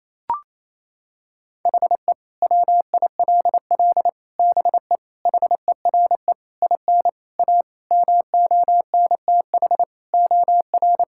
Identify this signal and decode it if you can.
Morse code, heard as a keyed tone, about 28 wpm, 700 Hz. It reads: HE WILL BE HERE IN A MONTH OR
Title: Sentences from Top 1000 Words - Review Part 5 - 28wpm